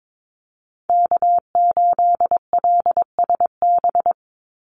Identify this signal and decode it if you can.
Morse code, heard as a keyed tone, about 22 wpm, 700 Hz. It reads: K8LSB
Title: Call Signs Repeated 3x - 22wpm